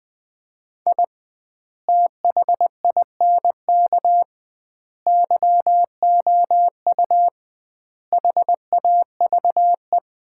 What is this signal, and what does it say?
Morse code, heard as a keyed tone, about 20 wpm, 700 Hz. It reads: I THINK YOU HAVE
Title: 2x Word-Spacing - Sets of 5 Words - 20wpm